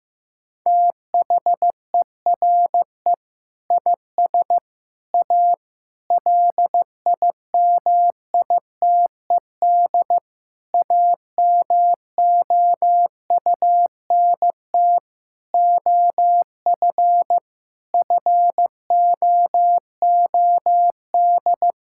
Morse code, heard as a keyed tone, about 15 wpm, 700 Hz. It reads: THERE IS A LIMITED AMOUNT OF FOOD
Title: Sentences from Top 1000 Words - Review Part 6 - 15wpm